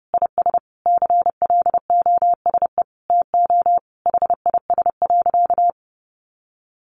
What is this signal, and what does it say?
Morse code, heard as a keyed tone, about 30 wpm, 700 Hz. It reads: IS CLOSE TO HIS.
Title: Sentences from Top 900 Words 30wpm